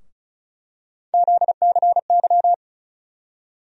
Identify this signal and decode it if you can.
Morse code, heard as a keyed tone, about 36 wpm, 700 Hz. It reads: ZCY
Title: USA Call Sign Postfixes - 3 Letters 35wpm